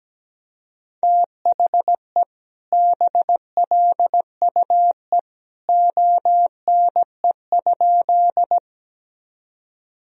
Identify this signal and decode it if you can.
Morse code, heard as a keyed tone, about 17 wpm, 700 Hz. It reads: THE BLUE ONE?
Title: Sentences from Top 1000 Words - Review Part 3 - 17wpm